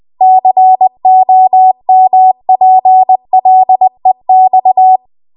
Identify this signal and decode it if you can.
Morse code, heard as a keyed tone, about 20 wpm, 750 Hz. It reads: COMPLEX
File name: Archive / 20 WPM / 20WPM